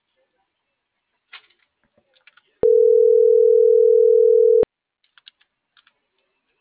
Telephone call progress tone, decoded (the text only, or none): ringback tone